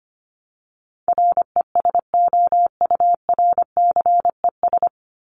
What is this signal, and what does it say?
Morse code, heard as a keyed tone, about 25 wpm, 700 Hz. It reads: RESOURCES
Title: ES:S Suffix - Top 100 Words with Base Words 25wpm